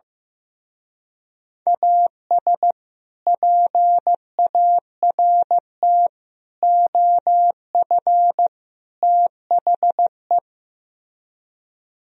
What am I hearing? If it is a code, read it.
Morse code, heard as a keyed tone, about 15 wpm, 700 Hz. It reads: AS PART OF THE
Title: Speed-Racing - Sets of 4 Words 15wpm